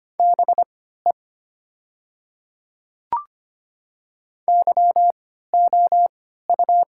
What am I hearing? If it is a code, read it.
Morse code, heard as a keyed tone, about 25 wpm, spaced as for 14 wpm, 700 Hz. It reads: BE YOU
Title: 3x Character Spacing - Top 200 Words - 25wpm